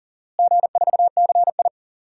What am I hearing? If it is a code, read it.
Morse code, heard as a keyed tone, about 40 wpm, 700 Hz. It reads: G4CI
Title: DXCC-100 Call Signs - 4 Characters - Repeated 2x - 40wpm